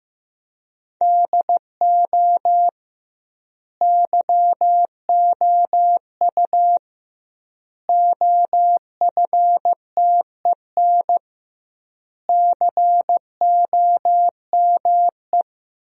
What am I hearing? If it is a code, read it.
Morse code, heard as a keyed tone, about 15 wpm, 700 Hz. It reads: DO YOU OFTEN COME
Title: 2x Word-Spacing - Sentences from Top 300 Words - 15wpm